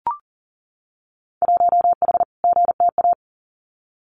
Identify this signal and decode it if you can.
Morse code, heard as a keyed tone, about 38 wpm, 700 Hz. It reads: 1H GTA
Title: Winter Field Day Exchanges Encore 40wpm